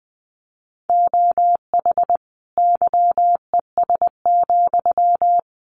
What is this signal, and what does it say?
Morse code, heard as a keyed tone, about 20 wpm, 700 Hz. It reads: OH YES,
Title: Sentences from Top 900 Words 20wpm